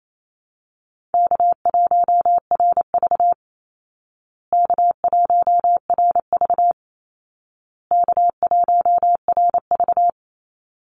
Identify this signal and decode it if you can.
Morse code, heard as a keyed tone, about 28 wpm, 700 Hz. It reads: K1RV K1RV K1RV
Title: Easy Call Signs Repeated 3x - 28wpm